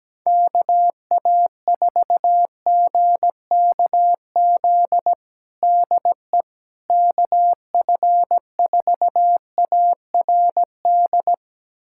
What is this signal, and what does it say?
Morse code, heard as a keyed tone, about 17 wpm, 700 Hz. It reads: KA4GKZ DE KF4ARD
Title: USA Callsign DE Callsign - 6 Characters - 17wpm